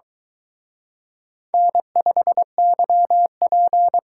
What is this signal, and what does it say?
Morse code, heard as a keyed tone, about 22 wpm, 700 Hz. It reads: N5YP